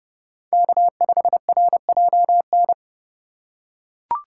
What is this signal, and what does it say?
Morse code, heard as a keyed tone, about 30 wpm, 700 Hz. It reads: K5RJN
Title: USA Call Signs - 5 Characters - 3x Character Spacing - 30wpm